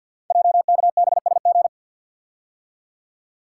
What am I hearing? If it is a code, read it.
Morse code, heard as a keyed tone, about 52 wpm, 690 Hz. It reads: JKBSG
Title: Code Groups - 5 Letters - 50wpm